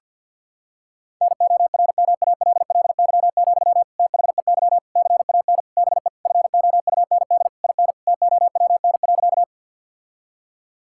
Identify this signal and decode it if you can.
Morse code, heard as a keyed tone, about 50 wpm, 700 Hz. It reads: NORMALLY, THEY CAN BE FOUND IN TOWN.
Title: Sentences from Top 1000 Words - Review Part 6 - 50wpm